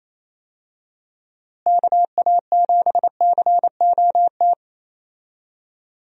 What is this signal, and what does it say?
Morse code, heard as a keyed tone, about 28 wpm, 700 Hz. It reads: KA7COT
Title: USA Call Signs - 6 Characters - Repeated 2x - 28wpm